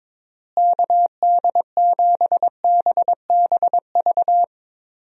Morse code, heard as a keyed tone, about 22 wpm, 700 Hz. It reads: KD7BBV
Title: USA Call Signs - 6 Characters - Repeated 2x - 22wpm